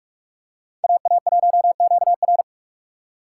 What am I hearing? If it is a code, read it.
Morse code, heard as a keyed tone, about 44 wpm, 700 Hz. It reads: AA1QR